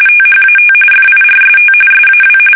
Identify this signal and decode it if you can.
RTTY, about 45.45 baud. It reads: THAT FIRST SAW
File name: fsk850 / usnavy5